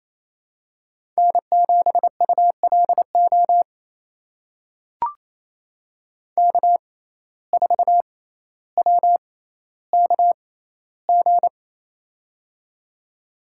Morse code, heard as a keyed tone, about 28 wpm, spaced as for 14 wpm, 700 Hz. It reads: N7ULO K4WKG
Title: USA Call Signs - 5 Characters - 6x Character Spacing - 28wpm